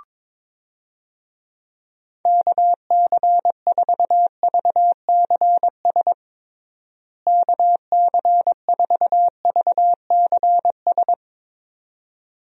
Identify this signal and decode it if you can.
Morse code, heard as a keyed tone, about 22 wpm, 700 Hz. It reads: KC4VCS KC4VCS
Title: USA Call Signs - 6 Characters - Repeated 2x - 22wpm